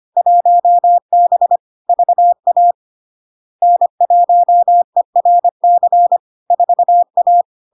Morse code, heard as a keyed tone, about 25 wpm, 700 Hz. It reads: 1B VA N1ERC 4A
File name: LARK-FD-2022-Log-in-CW-25wpm-700-Hz